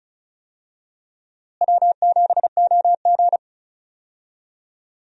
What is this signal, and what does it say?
Morse code, heard as a keyed tone, about 36 wpm, 700 Hz. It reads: W7OG
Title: USA Call Signs - 4 Characters - Repeated 3x - 35wpm